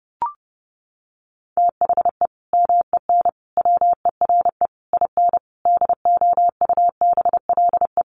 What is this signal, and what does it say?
Morse code, heard as a keyed tone, about 30 wpm, 700 Hz. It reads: THE MEN WERE IN DOUBLE
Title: Sentences from Top 1400 Words 30wpm